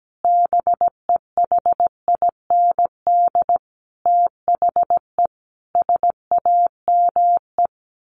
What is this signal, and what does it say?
Morse code, heard as a keyed tone, about 17 wpm, 700 Hz. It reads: BEHIND THE SAME
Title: A Sound Of Thunder 17wpm